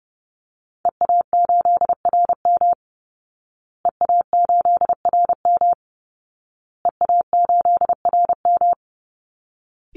Morse code, heard as a keyed tone, about 30 wpm, 700 Hz. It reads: EA8RM EA8RM EA8RM
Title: Call Signs Repeated 2x - 30wpm